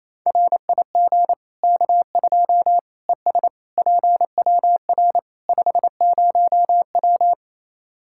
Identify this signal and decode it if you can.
Morse code, heard as a keyed tone, about 28 wpm, 700 Hz. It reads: RIG K2 ES PWR 50W
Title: Practice QSO - Exchange 2 - Info - 28wpm